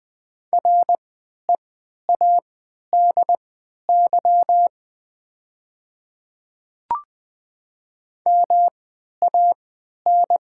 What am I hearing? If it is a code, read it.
Morse code, heard as a keyed tone, about 20 wpm, spaced as for 11 wpm, 700 Hz. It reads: READY MAN
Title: 3x Character Spacing - Top 500 Words - 20wpm